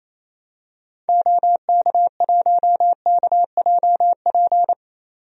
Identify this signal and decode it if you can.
Morse code, heard as a keyed tone, about 28 wpm, 700 Hz. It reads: OK1KJP
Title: DXCC-100 Call Signs - 6 Characters - 28wpm